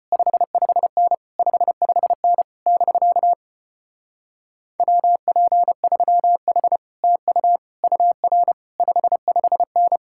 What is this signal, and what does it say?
Morse code, heard as a keyed tone, about 33 wpm, 700 Hz. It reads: 55N 55N <BK> WP3H TU UR 55N